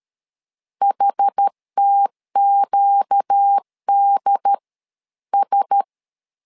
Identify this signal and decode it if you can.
Morse code, heard as a keyed tone, about 13 wpm, spaced as for 11 wpm, 800 Hz. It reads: HTQDS
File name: LEZIONI / 4 Q1S3 12wpm 2023